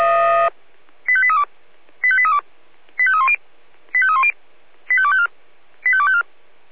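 Five-tone selective call (ZVEI), heard as quick runs of five tones, tones of about 70 ms. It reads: 85832 85832 85310 85310 85254 85254
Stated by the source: Five-tone ZVEI selective call received by SDR.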